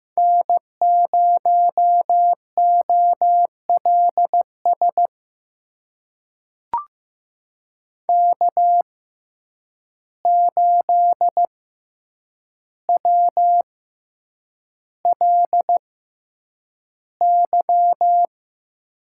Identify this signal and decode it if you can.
Morse code, heard as a keyed tone, about 15 wpm, spaced as for 8 wpm, 700 Hz. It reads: N0OLS K8WLY
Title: USA Call Signs - 5 Characters - 6x Character Spacing - 15wpm